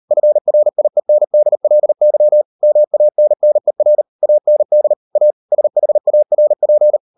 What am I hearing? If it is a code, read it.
Morse code, heard as a keyed tone, about 38 wpm, 600 Hz. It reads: FRIENDLY MANNER AND A SHARP